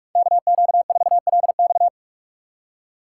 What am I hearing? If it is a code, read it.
Morse code, heard as a keyed tone, about 46 wpm, 700 Hz. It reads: KQ4LX
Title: USA Call Signs - 5 Characters - 2x Character Spacing - 45wpm